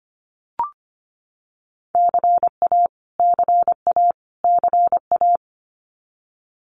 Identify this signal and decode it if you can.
Morse code, heard as a keyed tone, about 25 wpm, 700 Hz. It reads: CA CA CA
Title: US States Repeated 3x 25wpm